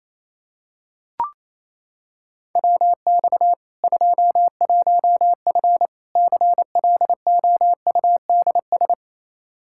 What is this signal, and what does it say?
Morse code, heard as a keyed tone, about 28 wpm, 700 Hz. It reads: WX 21F CLOUDS
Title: K1USN SST QSO Practice 28wpm